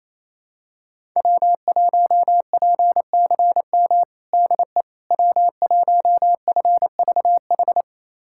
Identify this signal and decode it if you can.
Morse code, heard as a keyed tone, about 28 wpm, 700 Hz. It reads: W1PCM DE W1FVH
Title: USA Callsign DE Callsign - 5 Characters - 28wpm